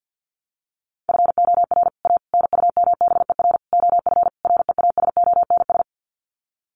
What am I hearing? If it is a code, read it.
Morse code, heard as a keyed tone, about 48 wpm, 700 Hz. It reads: FOR A NUMBER OF REASONS
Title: Sets of 5 Words Encore2 50wpm